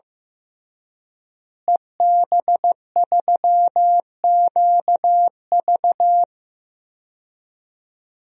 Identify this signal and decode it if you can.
Morse code, heard as a keyed tone, about 16 wpm, 700 Hz. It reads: EB3QV
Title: DXCC-100 Call Signs - 5 Characters - Repeated 3x - 15wpm